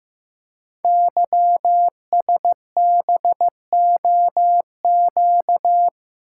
Morse code, heard as a keyed tone, about 15 wpm, 700 Hz. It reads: YSBOQ
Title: Code Groups - 5 Letters - 15wpm